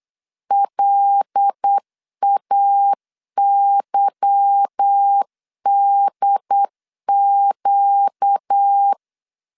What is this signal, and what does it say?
Morse code, heard as a keyed tone, about 8 wpm, 800 Hz. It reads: LAYDQ
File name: LEZIONI / 8 YJ45 / 8 Y 8wpm 3245